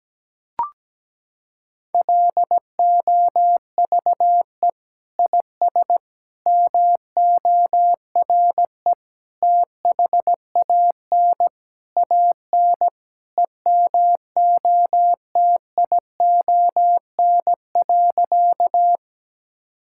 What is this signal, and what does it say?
Morse code, heard as a keyed tone, about 17 wpm, 700 Hz. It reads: LOVE IS MORE THAN AN EMOTION.
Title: Sentences from Top 1600 Words 17wpm